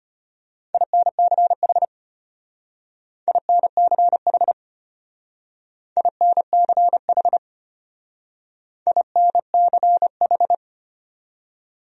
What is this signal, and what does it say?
Morse code, heard as a keyed tone, about 32 wpm, 700 Hz. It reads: INCH INCH INCH INCH